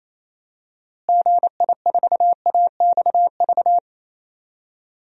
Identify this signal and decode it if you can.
Morse code, heard as a keyed tone, about 28 wpm, 700 Hz. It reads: GI4AXV
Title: DXCC-100 Call Signs - 6 Characters - Repeated 3x - 28wpm